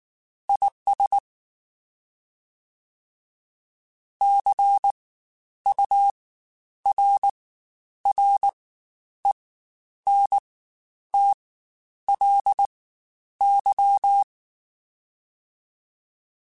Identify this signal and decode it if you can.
Morse code, heard as a keyed tone, about 19 wpm, spaced as for 9 wpm, 800 Hz. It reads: IS CURRENTLY
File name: Spaced Characters / Spaced Characters-mp3s / Weather5 22 sp @18